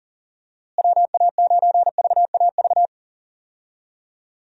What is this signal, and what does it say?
Morse code, heard as a keyed tone, about 40 wpm, 700 Hz. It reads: WA9VAV
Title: USA Call Signs - 6 Characters - Repeated 3x - 40wpm